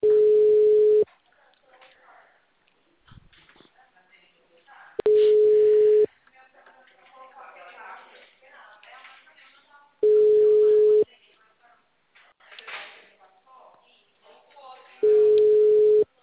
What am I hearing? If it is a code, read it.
Ringback tone (Europe, 425 Hz), with speech.